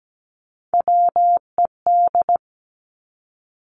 Morse code, heard as a keyed tone, about 17 wpm, 700 Hz. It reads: WED
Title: Three Challenging Letter-Number Combinations 17wpm